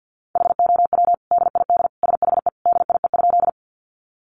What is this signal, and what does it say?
Morse code, heard as a keyed tone, about 50 wpm, 700 Hz. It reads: HOW DID SHE DIE?
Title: Sentences from Top 1300 Words 50wpm